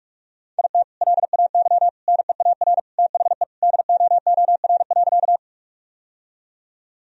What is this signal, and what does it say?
Morse code, heard as a keyed tone, about 43 wpm, 700 Hz. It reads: IT LAY NEAR THE DOOR.
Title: Sentences from Top 1000 Words - Review Part 2 - 45wpm